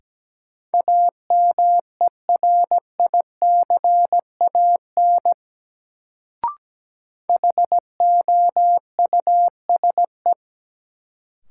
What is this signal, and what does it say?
Morse code, heard as a keyed tone, about 17 wpm, 700 Hz. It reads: AMERICAN HOUSE
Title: Sets of 2 Words - New Word Drill - 17wpm